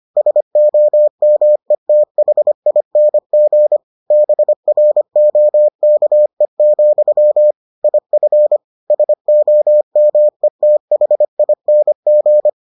Morse code, heard as a keyed tone, about 25 wpm, 600 Hz. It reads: SOMETHING BROKE, IF SOMETHING